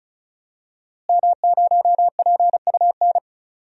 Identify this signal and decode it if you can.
Morse code, heard as a keyed tone, about 35 wpm, 700 Hz. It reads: M0PUN
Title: DXCC-100 Call Signs - 5 Characters - Repeated 2x - 35wpm